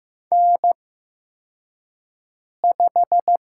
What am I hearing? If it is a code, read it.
Morse code, heard as a keyed tone, about 15 wpm, spaced as for 4 wpm, 700 Hz. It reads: N5
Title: USA Call Signs - 5 Characters - 8x Character Spacing - 15wpm